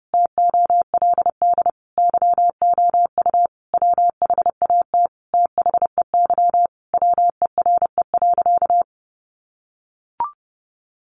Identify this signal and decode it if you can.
Morse code, heard as a keyed tone, about 30 wpm, 700 Hz. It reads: TOLD YOU WHAT THEY WERE.
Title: Sentences from Top 1000 Words 30wpm